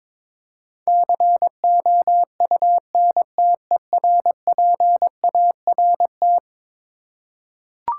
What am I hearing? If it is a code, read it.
Morse code, heard as a keyed tone, about 23 wpm, 700 Hz. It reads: COUNTERPART
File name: practice / 22wpm / Eleven Letter Words - 22wpm